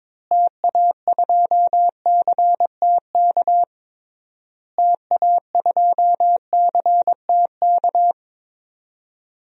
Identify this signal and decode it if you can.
Morse code, heard as a keyed tone, about 22 wpm, 700 Hz. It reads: TA2CTK TA2CTK
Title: DXCC-100 Call Signs - 6 Characters - Repeated 3x - 22wpm